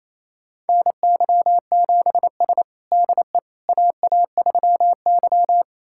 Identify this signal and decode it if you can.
Morse code, heard as a keyed tone, about 29 wpm, 690 Hz. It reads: NY7S DE AA3Y
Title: USA Callsign DE Callsign - 4 Characters - 28wpm